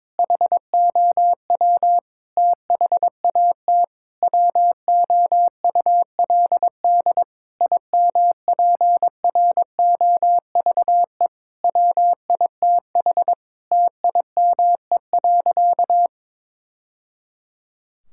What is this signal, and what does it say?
Morse code, heard as a keyed tone, about 22 wpm, 700 Hz. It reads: HOW THAT WOULD IMPROVE WITH TIME.
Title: Sentences from Top 1000 Words - Review Part 5 - 22wpm